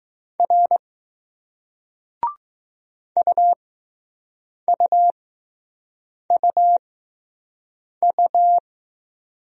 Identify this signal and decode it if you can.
Morse code, heard as a keyed tone, about 20 wpm, spaced as for 6 wpm, 700 Hz. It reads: R UUUU